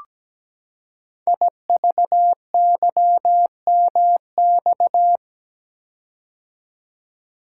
Morse code, heard as a keyed tone, about 17 wpm, 700 Hz. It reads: IVYMX
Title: Code Groups - 5 Letters - 17wpm